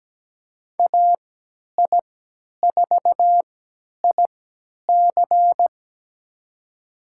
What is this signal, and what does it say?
Morse code, heard as a keyed tone, about 17 wpm, spaced as for 10 wpm, 700 Hz. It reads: AI4IC
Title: USA Call Signs - 5 Characters - 3x Character Spacing - 17wpm